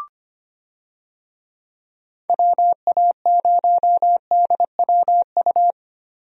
Morse code, heard as a keyed tone, about 25 wpm, 700 Hz. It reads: WA0DWU
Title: USA Call Signs - 6 Characters - 25wpm